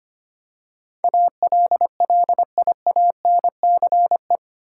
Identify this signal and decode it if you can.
Morse code, heard as a keyed tone, about 25 wpm, 700 Hz. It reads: ALLIANCE
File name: Eight Letter Words - 25wpm